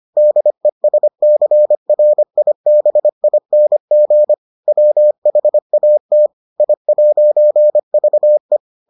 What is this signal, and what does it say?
Morse code, heard as a keyed tone, about 25 wpm, 600 Hz. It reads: DESCRIBING WHAT I'VE